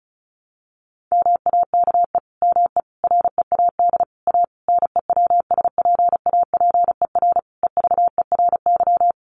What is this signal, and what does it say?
Morse code, heard as a keyed tone, about 34 wpm, 700 Hz. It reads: MAKE ME READ A NEWSPAPER EVERY
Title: Sentences from Top 1100 Words 35wpm